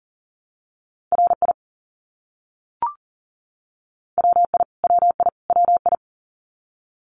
Morse code, heard as a keyed tone, about 40 wpm, 700 Hz. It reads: RI WI WI WI
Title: US States Repeated 3x 40wpm